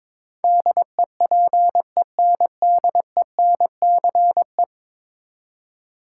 Morse code, heard as a keyed tone, about 22 wpm, 700 Hz. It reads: DEPENDENCE
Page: morse-code-ninja-practice-22wpm